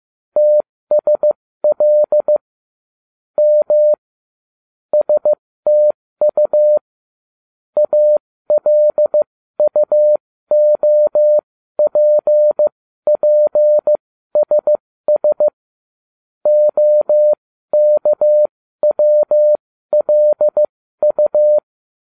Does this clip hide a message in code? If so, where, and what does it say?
Yes: Morse code from 0.4 to 21.6 s, heard as a keyed tone, about 15 wpm, spaced as for 13 wpm, 600 Hz. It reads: TSL M STU ALUOPPSS OKWLU